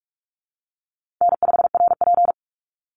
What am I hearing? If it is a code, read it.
Morse code, heard as a keyed tone, about 44 wpm, 700 Hz. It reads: N5RP
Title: Call Signs Repeated 3x - 45wpm